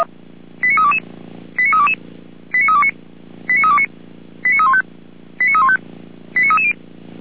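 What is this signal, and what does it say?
Five-tone selective call (ZVEI), heard as quick runs of five tones, tones of about 70 ms. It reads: 89322 89322 89329 89329 89315 89315 89330
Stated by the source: Five-tone ZVEI selective call received on an SDR.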